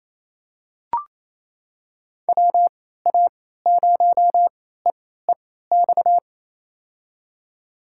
Morse code, heard as a keyed tone, about 28 wpm, spaced as for 16 wpm, 700 Hz. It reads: WA0EEX
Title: USA Call Signs - 6 Characters - 3x Character Spacing - 28wpm